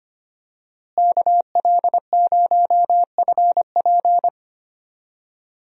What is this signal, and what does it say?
Morse code, heard as a keyed tone, about 25 wpm, 700 Hz. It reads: KL0FP
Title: USA Call Signs - 5 Characters - Repeated 2x - 25wpm